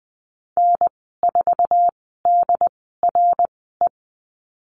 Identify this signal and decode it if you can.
Morse code, heard as a keyed tone, about 20 wpm, spaced as for 15 wpm, 700 Hz. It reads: N4DRE
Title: 069 Lesson 12g - Letter d - Call Signs 20wpm